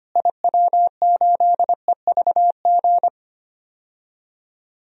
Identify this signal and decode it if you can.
Morse code, heard as a keyed tone, about 25 wpm, 700 Hz. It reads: IW8EVG
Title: DXCC-100 Call Signs - 6 Characters - Repeated 2x - 25wpm